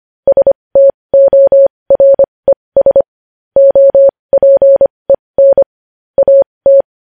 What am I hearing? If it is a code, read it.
Morse code, heard as a keyed tone, about 25 wpm, spaced as for 20 wpm, 550 Hz. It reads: STORES OPEN AT
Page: news-headlines-in-morse-code-at-20-wpm